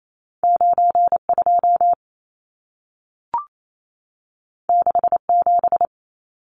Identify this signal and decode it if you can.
Morse code, heard as a keyed tone, about 28 wpm, 700 Hz. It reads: 92 67
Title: Numbers - 2-Digit 28wpm